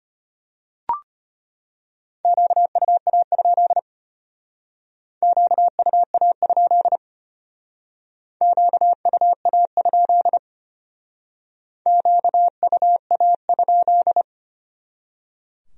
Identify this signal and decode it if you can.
Morse code, heard as a keyed tone, about 32 wpm, 700 Hz. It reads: QUA? QUA? QUA? QUA?